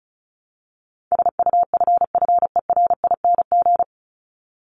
Morse code, heard as a keyed tone, about 34 wpm, 700 Hz. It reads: SUFFERING
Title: ING Suffix - Top 100 Words with Base Words 35wpm